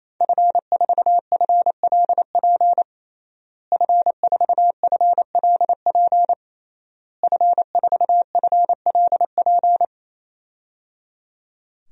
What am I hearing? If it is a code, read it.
Morse code, heard as a keyed tone, about 27 wpm, 700 Hz. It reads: F4FLP F4FLP F4FLP
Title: DXCC-100 Call Signs - 5 Characters - Repeated 3x - 28wpm